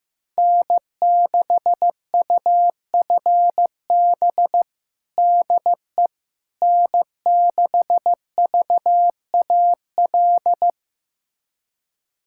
Morse code, heard as a keyed tone, about 15 wpm, 700 Hz. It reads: N6UFB DE N6VAL
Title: USA Callsign DE Callsign - 5 Characters - 15wpm